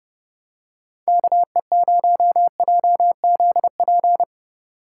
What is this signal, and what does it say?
Morse code, heard as a keyed tone, about 30 wpm, 700 Hz. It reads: KE0JZP